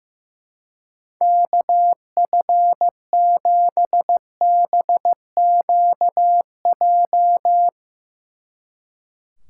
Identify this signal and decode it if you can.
Morse code, heard as a keyed tone, about 16 wpm, 700 Hz. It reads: KF7BQJ
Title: Speed Racing - USA 2x3 Call Signs 15wpm